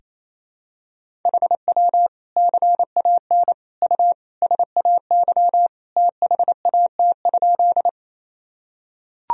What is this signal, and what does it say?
Morse code, heard as a keyed tone, about 28 wpm, 700 Hz. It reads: HW CAN U SAY THAT?
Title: Sentences from Top 200 Words and Abbreviations 28wpm